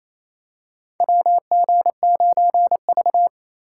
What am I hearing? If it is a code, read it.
Morse code, heard as a keyed tone, about 28 wpm, 700 Hz. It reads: WG9V
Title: USA Call Signs - 4 Characters - 4x Character Spacing - 28wpm